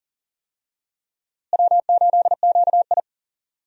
Morse code, heard as a keyed tone, about 40 wpm, 700 Hz. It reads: W8QI
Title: USA Call Signs - 4 Characters - Repeated 3x - 40wpm